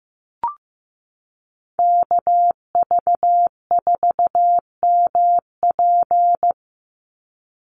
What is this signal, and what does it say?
Morse code, heard as a keyed tone, about 15 wpm, 700 Hz. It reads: KV4MP
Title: USA Call Signs 15wpm